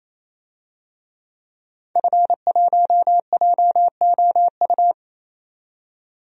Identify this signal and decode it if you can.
Morse code, heard as a keyed tone, about 28 wpm, 700 Hz. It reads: F1JOU
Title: DXCC-100 Call Signs - 5 Characters - 28wpm